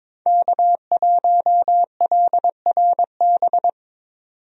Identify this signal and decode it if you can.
Morse code, heard as a keyed tone, about 22 wpm, 700 Hz. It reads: K1LRB
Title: USA Call Signs - 5 Characters - 22wpm